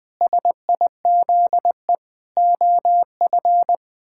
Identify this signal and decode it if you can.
Morse code, heard as a keyed tone, about 20 wpm, 700 Hz. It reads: SIZE OF
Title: Sentences with 7-Word Phrases - 20wpm